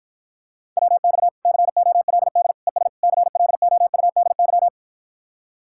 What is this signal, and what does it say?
Morse code, heard as a keyed tone, about 54 wpm, 700 Hz. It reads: WX COLD ES CLOUDY